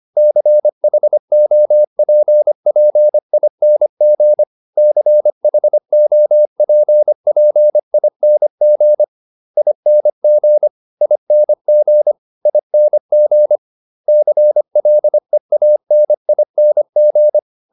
Morse code, heard as a keyed tone, about 25 wpm, 600 Hz. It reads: CHOPPING CHOPPING ING ING ING CLEANING